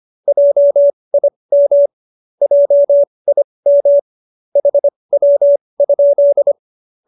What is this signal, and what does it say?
Morse code, heard as a keyed tone, about 25 wpm, spaced as for 20 wpm, 570 Hz. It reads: JIM JIM HW?
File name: QSO303 20